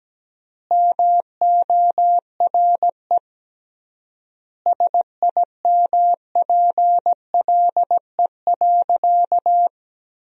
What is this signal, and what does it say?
Morse code, heard as a keyed tone, about 17 wpm, 700 Hz. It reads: MORE SIMPLE.
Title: 3x Word-Spacing - Sentences from Top 400 Words - 17wpm